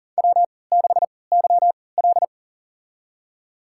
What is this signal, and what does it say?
Morse code, heard as a keyed tone, about 40 wpm, spaced as for 23 wpm, 700 Hz. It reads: W6YL
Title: USA Call Signs - 4 Characters - 3x Character Spacing - 40wpm